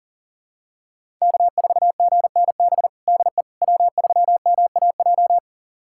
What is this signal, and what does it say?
Morse code, heard as a keyed tone, about 40 wpm, 700 Hz. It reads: K4GNB DE W3MAJ
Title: USA Callsign DE Callsign - 5 Characters - 40wpm